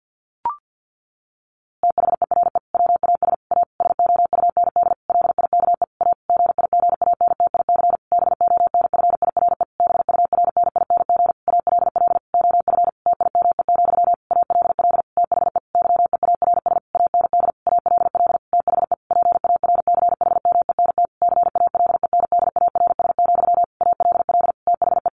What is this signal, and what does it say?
Morse code, heard as a keyed tone, about 50 wpm, 700 Hz. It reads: THERE WAS A SOUND LIKE A GIGANTIC BONFIRE BURNING ALL OF TIME, ALL THE YEARS AND ALL THE PARCHMENT CALENDARS, ALL THE
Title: A Sound Of Thunder 50wpm